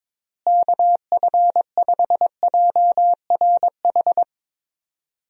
Morse code, heard as a keyed tone, about 22 wpm, 700 Hz. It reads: KF5JRH
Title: USA Call Signs - 6 Characters - Repeated 2x - 22wpm